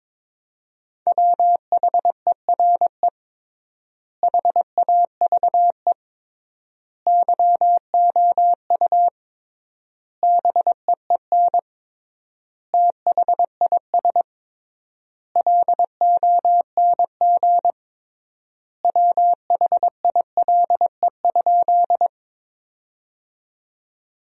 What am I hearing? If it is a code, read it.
Morse code, heard as a keyed tone, about 22 wpm, 700 Hz. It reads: WHERE HAVE YOU BEEN THIS LONG WHILE?
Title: 3x Word-Spacing - Sentences from Top 300 Words - 22wpm